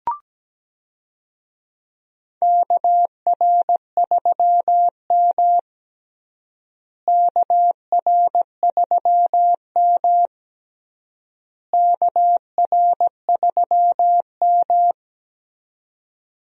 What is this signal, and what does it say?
Morse code, heard as a keyed tone, about 17 wpm, 700 Hz. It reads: KR3M KR3M KR3M
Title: USA Call Signs - 4 Characters - Repeated 3x - 17wpm